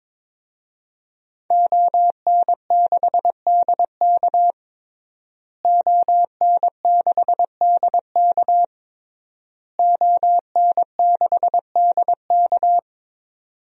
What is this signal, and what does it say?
Morse code, heard as a keyed tone, about 22 wpm, 700 Hz. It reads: ON6DK ON6DK ON6DK
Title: DXCC-100 Call Signs - 5 Characters - Repeated 3x - 22wpm